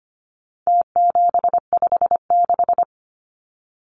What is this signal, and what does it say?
Morse code, heard as a keyed tone, about 25 wpm, 700 Hz. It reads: T756
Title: Numbers - Contest Style - 4-Digit 25wpm